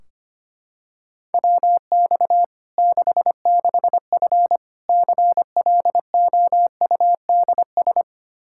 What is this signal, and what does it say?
Morse code, heard as a keyed tone, about 25 wpm, 700 Hz. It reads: WX 66F CLOUDS
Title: Practice QSO - Exchange 2 - Info - 25wpm